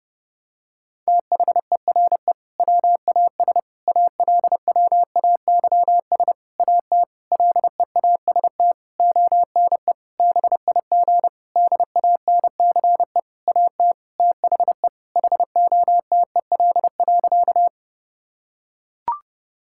Morse code, heard as a keyed tone, about 30 wpm, 700 Hz. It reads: THERE WAS ALWAYS AT LEAST ONE BIG DANCE AT THE HOTEL.
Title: Sentences from Top 1600 Words 30wpm